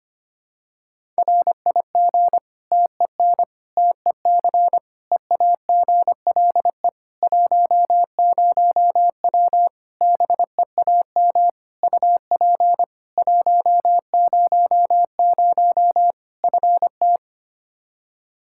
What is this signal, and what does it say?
Morse code, heard as a keyed tone, about 25 wpm, 700 Hz. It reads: RIG TEN TEC EAGLE 10W BEAM UP 100 FT
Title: Practice QSO - Exchange 2 - Info - 25wpm